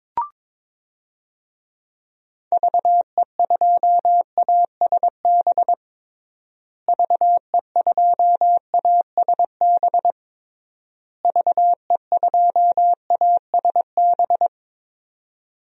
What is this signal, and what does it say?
Morse code, heard as a keyed tone, about 22 wpm, 700 Hz. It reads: VE2ASB VE2ASB VE2ASB
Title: DXCC-100 Call Signs - 6 Characters - Repeated 3x - 22wpm